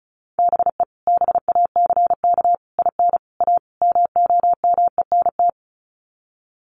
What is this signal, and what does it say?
Morse code, heard as a keyed tone, about 34 wpm, 700 Hz. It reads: BE BACK IN A MOMENT
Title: Sets of 5 Words 35wpm